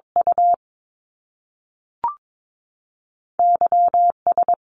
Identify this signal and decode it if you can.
Morse code, heard as a keyed tone, about 22 wpm, spaced as for 3 wpm, 700 Hz. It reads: UY S
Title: Top Two Letter Combinations in Words 22wpm